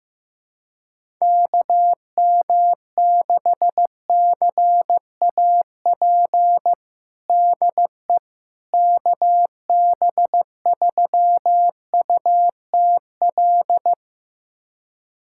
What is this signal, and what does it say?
Morse code, heard as a keyed tone, about 15 wpm, 700 Hz. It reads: KM6CAP DE KB3UTL
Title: USA Callsign DE Callsign 15wpm